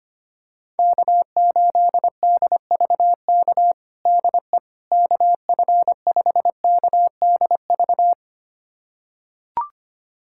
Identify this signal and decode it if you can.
Morse code, heard as a keyed tone, about 25 wpm, 700 Hz. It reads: K8DVK DE KF5KDV
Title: USA Callsign DE Callsign 25wpm